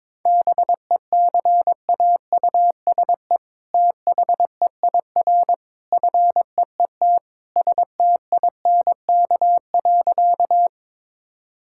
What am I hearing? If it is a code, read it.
Morse code, heard as a keyed tone, about 22 wpm, 700 Hz. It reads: BECAUSE THEIR FEET STINK.
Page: morse-code-ninja-practice-22wpm